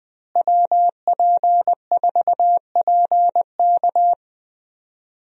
Morse code, heard as a keyed tone, about 20 wpm, 700 Hz. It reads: WP4PK
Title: USA Call Signs - 5 Characters - Repeated 2x - 20wpm